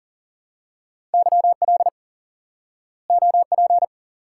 Morse code, heard as a keyed tone, about 39 wpm, 700 Hz. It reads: YL OP
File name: course / instant-qso-element-course-3 / IQE Course 3 - Lesson 89 - YAGI - Two Elements - 40wpm